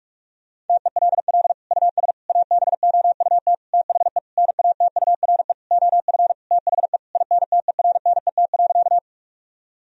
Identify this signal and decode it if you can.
Morse code, heard as a keyed tone, about 41 wpm, 700 Hz. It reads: TELL US ABOUT THE NATURE OF THE INTERNET.
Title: Sentences from Top 1000 Words - Review Part 4 - 45wpm